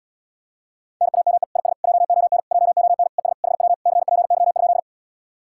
Morse code, heard as a keyed tone, about 80 wpm, 700 Hz. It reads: NAME IS JON JON ES HW CPY?